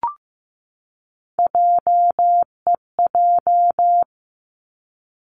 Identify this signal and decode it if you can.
Morse code, heard as a keyed tone, about 15 wpm, 700 Hz. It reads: JEJ
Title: Three Challenging Letter-Number Combinations 15wpm